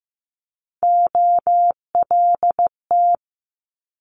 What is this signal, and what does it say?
Morse code, heard as a keyed tone, about 15 wpm, 700 Hz. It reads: OLT
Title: Multi-letter Phonemes Quick Fire 15wpm